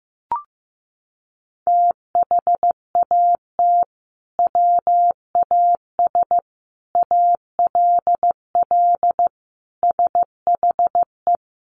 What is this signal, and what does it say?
Morse code, heard as a keyed tone, about 15 wpm, 700 Hz. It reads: THAT WAS ALL SHE
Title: Sentences from Top 100 Words 15wpm